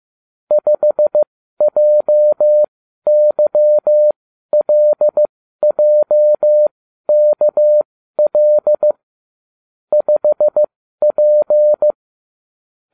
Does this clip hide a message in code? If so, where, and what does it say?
Yes: Morse code from 0.5 to 11.9 s, heard as a keyed tone, about 15 wpm, spaced as for 12 wpm, 600 Hz. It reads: 5JYLJKL 5P